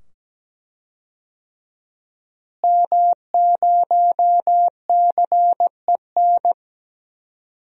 Morse code, heard as a keyed tone, about 17 wpm, 700 Hz. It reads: M0CEN
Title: DXCC-100 Call Signs - 5 Characters - Repeated 2x - 17wpm